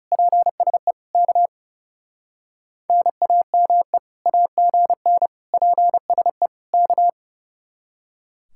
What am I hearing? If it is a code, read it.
Morse code, heard as a keyed tone, about 37 wpm, 700 Hz. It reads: PSE K NAME AGN PSE K
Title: Speed-Racing Common QSO Phrases - 30wpm